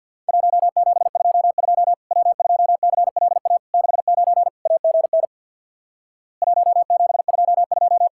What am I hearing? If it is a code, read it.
Morse code, heard as a keyed tone, about 50 wpm, 700 Hz. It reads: 1722 W2CLA 69 AGN 1722
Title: Worked All Europe DX Contest - QTC Exchange with Repeats - 50wpm